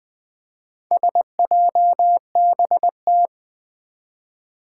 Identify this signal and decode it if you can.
Morse code, heard as a keyed tone, about 20 wpm, 700 Hz. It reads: SJBT
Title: Four Challenging Letter-Number Combinations 20wpm